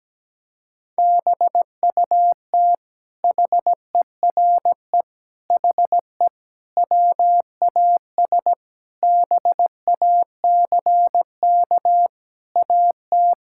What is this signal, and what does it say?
Morse code, heard as a keyed tone, about 17 wpm, 700 Hz. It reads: BUT HERE HE WAS BACK AT
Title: Sentences from Top 1600 Words 17wpm